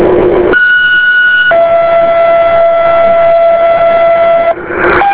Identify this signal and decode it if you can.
Two-tone page, heard as a one-second tone, then a three-second tone, about 1460 then 670 Hz, with noise. It is heard from 0.5 to 4.5 s.